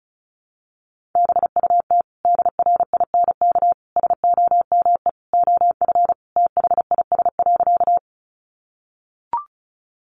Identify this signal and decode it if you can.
Morse code, heard as a keyed tone, about 34 wpm, 700 Hz. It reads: BUT DRINK SOME OF THIS.
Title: Sentences from Top 1500 Words 35wpm